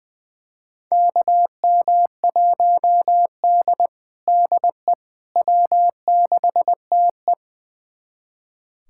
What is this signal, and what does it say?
Morse code, heard as a keyed tone, about 20 wpm, 700 Hz. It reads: KM1D DE W6TE
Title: USA Callsign DE Callsign - 4 Characters - 20wpm